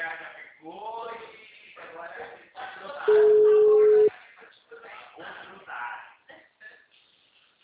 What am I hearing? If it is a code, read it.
Ringback tone (Europe, 425 Hz), with speech.